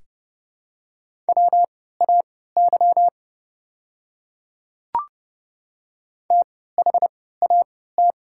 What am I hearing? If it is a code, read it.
Morse code, heard as a keyed tone, about 30 wpm, spaced as for 17 wpm, 700 Hz. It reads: WAY THAT
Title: 3x Character Spacing - Top 300 Words - 30wpm